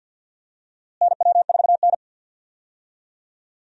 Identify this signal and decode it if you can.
Morse code, heard as a keyed tone, about 51 wpm, 700 Hz. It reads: NW4N
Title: USA Call Signs - 4 Characters - Repeated 2x - 50wpm